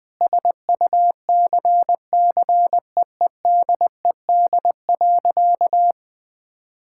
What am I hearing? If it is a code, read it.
Morse code, heard as a keyed tone, about 20 wpm, 700 Hz. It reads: SUCCEEDED.
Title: Sentences with 7-Word Phrases - 20wpm